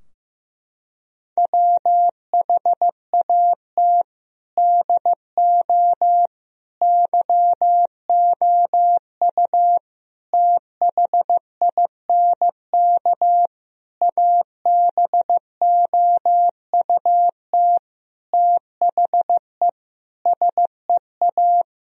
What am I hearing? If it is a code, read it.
Morse code, heard as a keyed tone, about 16 wpm, 700 Hz. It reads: WHAT DO YOU THINK ABOUT THE SEA
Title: Sentences from Top 1000 Words - Review Part 4 - 15wpm